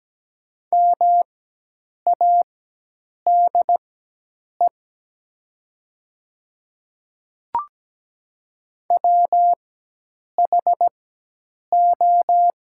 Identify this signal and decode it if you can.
Morse code, heard as a keyed tone, about 17 wpm, spaced as for 8 wpm, 700 Hz. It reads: MADE WHO
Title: 4x Character Spacing - Top 200 Words - 17wpm